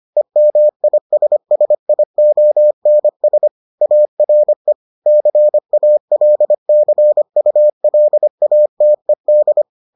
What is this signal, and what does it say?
Morse code, heard as a keyed tone, about 25 wpm, 600 Hz. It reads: EMISSIONS ARE CALCULATED